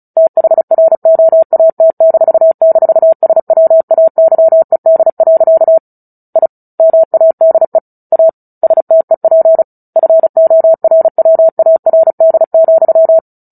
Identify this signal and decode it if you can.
Morse code, heard as a keyed tone, about 34 wpm, 650 Hz. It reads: THROAT--SWAYED. I MADE A STEP FORWARD,
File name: Book Two / 09-Chapter 9 - 35 WPM